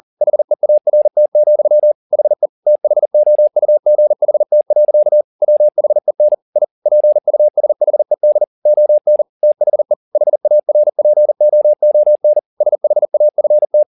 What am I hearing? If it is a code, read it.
Morse code, heard as a keyed tone, about 40 wpm, 600 Hz. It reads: HEART, HE THOUGHT. WHEN I PUSHED ON THE HARPOON SHAFT
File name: the old man and the sea 2 40wpm